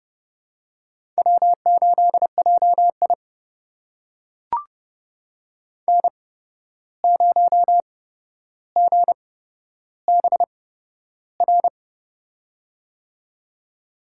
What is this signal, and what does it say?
Morse code, heard as a keyed tone, about 30 wpm, spaced as for 12 wpm, 700 Hz. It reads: W8JI N0GBR
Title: USA Call Signs - 8x Character Spacing - 30wpm